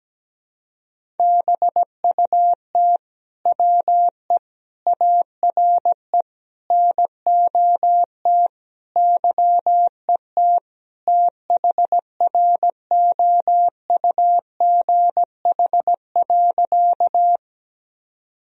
Morse code, heard as a keyed tone, about 17 wpm, 700 Hz. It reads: BUT WE ARE NOT YET THROUGH.
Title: Sentences from Top 1000 Words - Review Part 1 - 17wpm